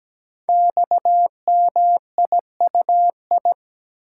Morse code, heard as a keyed tone, about 17 wpm, 700 Hz. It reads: XMIUI
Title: Code Groups - 5 Letters - 17wpm